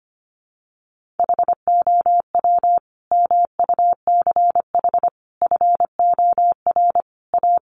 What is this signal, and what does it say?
Morse code, heard as a keyed tone, about 25 wpm, 700 Hz. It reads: HOW MUCH FOR A
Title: Sentences from Top 1400 Words 25wpm